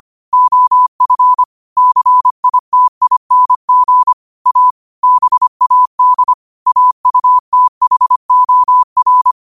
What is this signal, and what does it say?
Morse code, heard as a keyed tone, about 25 wpm, 1000 Hz. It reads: OF CITING A BAD AUTHOR
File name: qotdfiles / 25wpm